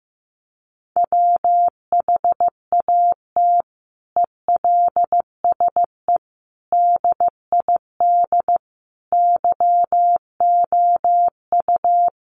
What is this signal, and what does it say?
Morse code, heard as a keyed tone, about 14 wpm, 700 Hz. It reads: WHAT ELSE DID YOU
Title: Sentences from Top 800 Words 15wpm